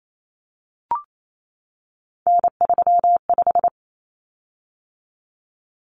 Morse code, heard as a keyed tone, about 28 wpm, 700 Hz. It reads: N35
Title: Common Cut Numbers - 3-Digit 28wpm